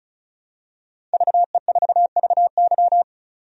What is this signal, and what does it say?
Morse code, heard as a keyed tone, about 36 wpm, 700 Hz. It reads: VE4VY
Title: DXCC-100 Call Signs - 5 Characters - 35wpm